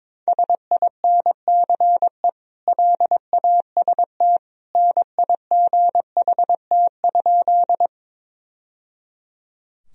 Morse code, heard as a keyed tone, about 22 wpm, 700 Hz. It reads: SINCE LAST NIGHT?
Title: Sentences from Top 1000 Words - Review Part 1 - 22wpm